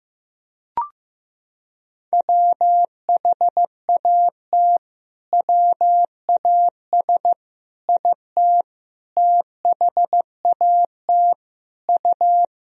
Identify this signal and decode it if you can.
Morse code, heard as a keyed tone, about 15 wpm, 700 Hz. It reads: WHAT WAS IT THAT U
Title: Sentences from Top 300 Words and Abbreviations 15wpm